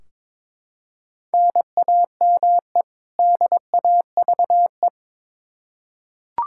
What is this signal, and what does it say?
Morse code, heard as a keyed tone, about 22 wpm, 700 Hz. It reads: NAME DAVE
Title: Practice QSO - Exchange 1 - Beginning - 22wpm